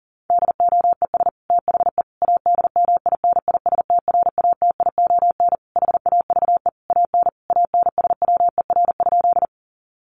Morse code, heard as a keyed tone, about 40 wpm, 700 Hz. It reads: DOES THE ADMINISTRATION HAVE AN ANSWER?
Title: Sentences from Top 1300 Words 40wpm